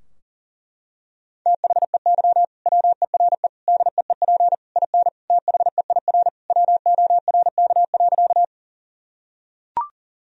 Morse code, heard as a keyed tone, about 40 wpm, 700 Hz. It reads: THEY WERE DEEP IN THEIR WORK.
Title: Sentences from Top 1000 Words - Review Part 3 - 40wpm